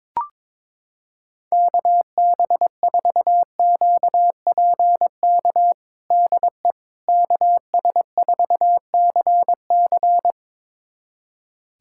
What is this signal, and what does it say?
Morse code, heard as a keyed tone, about 22 wpm, 700 Hz. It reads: KB4QPK DE KS4CC
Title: USA Callsign DE Callsign 22wpm